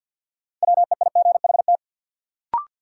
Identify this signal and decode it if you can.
Morse code, heard as a keyed tone, about 52 wpm, 700 Hz. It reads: WEIGHT
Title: Sentences from Top 800 Words - New Word Drill - 50wpm